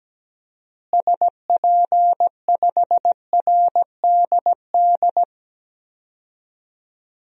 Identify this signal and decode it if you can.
Morse code, heard as a keyed tone, about 17 wpm, 700 Hz. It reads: SP5RDD
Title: DXCC-100 Call Signs - 6 Characters - Repeated 2x - 17wpm